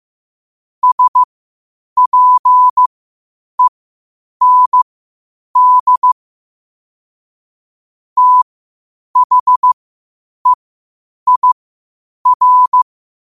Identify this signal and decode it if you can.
Morse code, heard as a keyed tone, about 15 wpm, spaced as for 8 wpm, 1000 Hz. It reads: SPEND THEIR
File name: qotdfiles / 8wpm